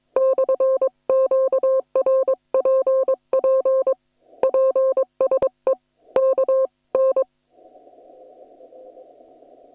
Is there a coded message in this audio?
Yes: Morse code from 0.2 to 7.2 s, heard as a keyed tone, about 22 wpm, 530 Hz. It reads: /QRPP PSE KN